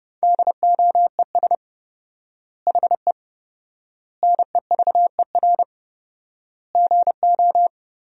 Morse code, heard as a keyed tone, about 30 wpm, 700 Hz. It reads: DOES HE NEVER GO
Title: 4x Word-Spacing - Sentences from Top 300 Words - 30wpm